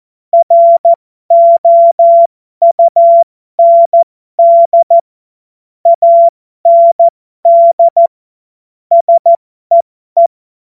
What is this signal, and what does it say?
Morse code, heard as a keyed tone, about 14 wpm, spaced as for 12 wpm, 680 Hz. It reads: ROUND AND SEE
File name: Morsekurs / Witze